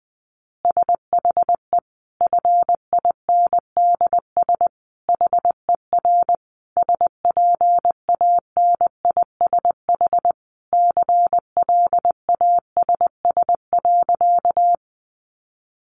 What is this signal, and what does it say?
Morse code, heard as a keyed tone, about 20 wpm, 700 Hz. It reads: SHE FINDS HER SPANISH CLASS.
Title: Additional Easy Phrases 20wpm